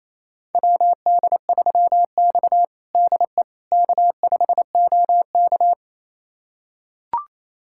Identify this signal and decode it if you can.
Morse code, heard as a keyed tone, about 28 wpm, 700 Hz. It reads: WD3X DE K5OK
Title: USA Callsign DE Callsign - 4 Characters - 28wpm